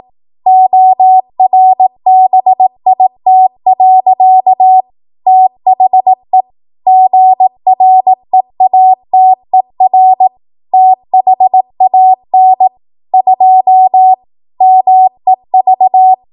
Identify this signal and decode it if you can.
Morse code, heard as a keyed tone, about 18 wpm, 750 Hz. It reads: ORBIT. THE GREATER THAN 2 MEV
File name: Morse / 18WPM